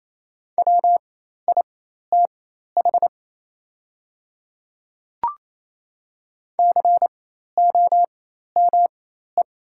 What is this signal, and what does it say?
Morse code, heard as a keyed tone, about 27 wpm, spaced as for 13 wpm, 690 Hz. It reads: WITH COME
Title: 4x Character Spacing - Top 100 Words - 28wpm